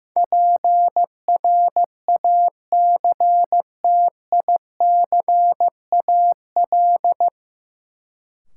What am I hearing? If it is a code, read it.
Morse code, heard as a keyed tone, about 15 wpm, 700 Hz. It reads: PRACTICAL
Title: Sentences from Top 1400 Words - New Word Drill Alt - 15wpm